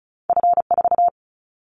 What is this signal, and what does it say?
Morse code, heard as a keyed tone, about 36 wpm, 700 Hz. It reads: F4
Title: Callsign Prefixes for Top 50 DXCC - 35wpm